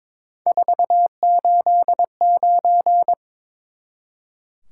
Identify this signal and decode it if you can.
Morse code, heard as a keyed tone, about 22 wpm, 700 Hz. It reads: 489
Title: Top 180 QSO Elements 22wpm